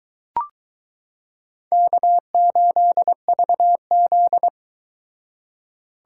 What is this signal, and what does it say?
Morse code, heard as a keyed tone, about 22 wpm, 700 Hz. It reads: K8VZ